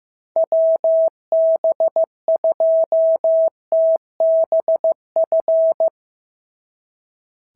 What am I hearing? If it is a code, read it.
Morse code, heard as a keyed tone, about 14 wpm, 650 Hz. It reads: WB2TBF
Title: SOTA QSO Practice 15wpm